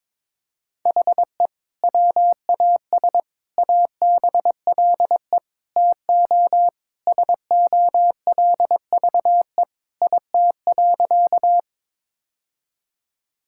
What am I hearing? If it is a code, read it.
Morse code, heard as a keyed tone, about 22 wpm, 700 Hz. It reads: HE WAS ABLE TO SOLVE IT.
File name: practice / 22wpm / Sentences with 3-Word Phrases - 22wpm